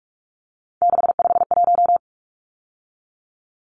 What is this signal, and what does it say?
Morse code, heard as a keyed tone, about 44 wpm, 700 Hz. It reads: 651
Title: Numbers - 3-Digit 45wpm